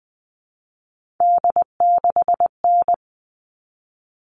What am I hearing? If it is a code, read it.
Morse code, heard as a keyed tone, about 20 wpm, 700 Hz. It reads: D6N
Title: All Cut Numbers - 3-Digit 20wpm